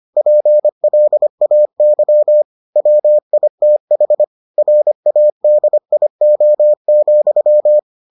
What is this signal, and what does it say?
Morse code, heard as a keyed tone, about 25 wpm, 600 Hz. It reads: PLAY WITH RADIO,